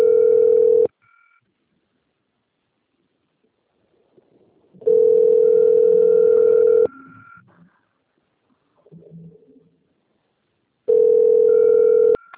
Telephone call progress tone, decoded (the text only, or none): ringback tone